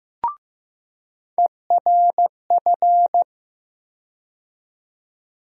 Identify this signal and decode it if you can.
Morse code, heard as a keyed tone, about 15 wpm, 700 Hz. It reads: ERF
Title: USA Call Sign Postfixes - 3 Letters 15wpm